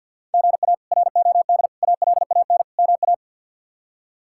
Morse code, heard as a keyed tone, about 52 wpm, 700 Hz. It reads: GA ROD ALAN MA
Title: K1USN SST QSO Practice - 50wpm